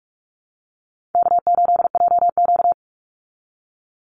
Morse code, heard as a keyed tone, about 44 wpm, 700 Hz. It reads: K8JQ
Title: Call Signs Repeated 3x - 45wpm